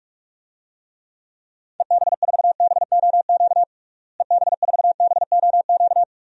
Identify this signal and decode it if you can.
Morse code, heard as a keyed tone, about 42 wpm, 700 Hz. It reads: EB4BOQ EB4BOQ
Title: DXCC-100 Call Signs - 6 Characters - Repeated 3x - 45wpm